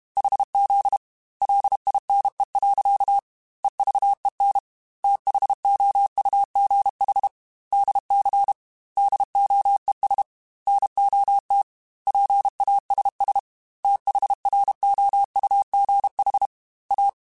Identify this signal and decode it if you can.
Morse code, heard as a keyed tone, about 32 wpm, 800 Hz. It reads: HZ LINE. EVEN THOUGH DC DOES NOT PASS THROUGH A
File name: Technical Subjects / Technical Subjects-mp3s / Sw Pwr Sup @30